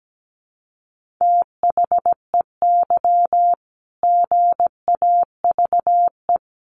Morse code, heard as a keyed tone, about 17 wpm, 700 Hz. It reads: THEY GAVE
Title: Sentences from Top 1400 Words 17wpm